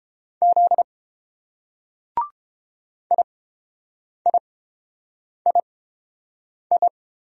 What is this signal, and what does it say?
Morse code, heard as a keyed tone, about 30 wpm, spaced as for 7 wpm, 700 Hz. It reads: Z IIII